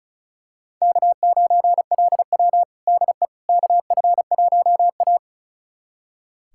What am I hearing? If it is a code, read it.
Morse code, heard as a keyed tone, about 36 wpm, 700 Hz. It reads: K9LW DE KF1A
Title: USA Callsign DE Callsign - 4 Characters - 35wpm